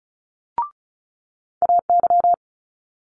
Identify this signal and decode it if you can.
Morse code, heard as a keyed tone, about 36 wpm, 700 Hz. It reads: AY
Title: Multi-letter Phonemes Quick Fire 35wpm